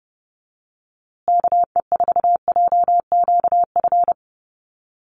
Morse code, heard as a keyed tone, about 30 wpm, 700 Hz. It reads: KE4JQF
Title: USA Call Signs Repeated 3x 30wpm